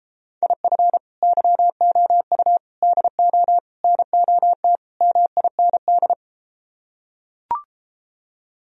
Morse code, heard as a keyed tone, about 33 wpm, 700 Hz. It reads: IF YOU DO NOT MIND